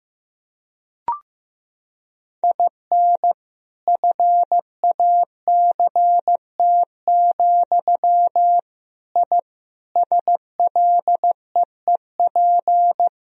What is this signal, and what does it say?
Morse code, heard as a keyed tone, about 16 wpm, 700 Hz. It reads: IN FACT, I SLEEP
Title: Sentences from Top 1000 Words - Review Part 3 - 15wpm